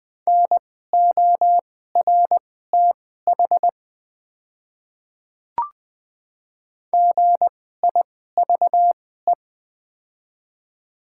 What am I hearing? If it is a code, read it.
Morse code, heard as a keyed tone, about 20 wpm, spaced as for 15 wpm, 700 Hz. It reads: NORTH GIVE
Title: 2x Character Spacing - Top 300 Words - 20wpm